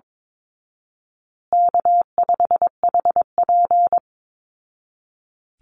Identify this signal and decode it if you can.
Morse code, heard as a keyed tone, about 22 wpm, 700 Hz. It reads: K5HP
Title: Call Signs Repeated 2x - 22wpm